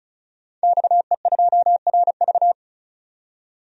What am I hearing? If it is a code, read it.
Morse code, heard as a keyed tone, about 34 wpm, 700 Hz. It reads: XE2RV
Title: DXCC-100 Call Signs - 5 Characters - 35wpm